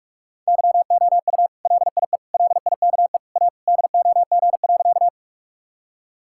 Morse code, heard as a keyed tone, about 46 wpm, 700 Hz. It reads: YOU LIE LIKE A DOG.
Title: Sentences from Top 1600 Words 45wpm